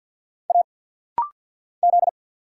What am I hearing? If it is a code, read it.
Morse code, heard as a keyed tone, about 50 wpm, 700 Hz. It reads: A Z
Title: Single Letter-Number - Rapid-Fire 50wpm